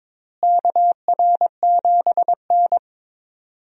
Morse code, heard as a keyed tone, about 22 wpm, 700 Hz. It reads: KR7N
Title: USA Call Signs - 4 Characters - Repeated 2x - 22wpm